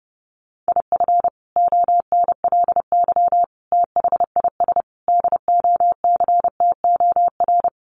Morse code, heard as a keyed tone, about 30 wpm, 700 Hz. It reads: IF ONLY THIS DOCTOR
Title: Sentences from Top 1500 Words 30wpm